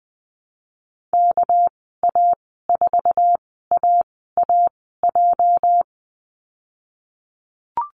Morse code, heard as a keyed tone, about 20 wpm, spaced as for 14 wpm, 700 Hz. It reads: KA4AAJ
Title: 231 Lesson 37g - Letter x - Call Signs 20wpm